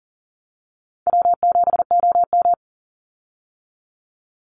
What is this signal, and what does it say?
Morse code, heard as a keyed tone, about 40 wpm, 700 Hz. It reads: W7OM
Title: Easy Call Signs Repeated 2x - 40wpm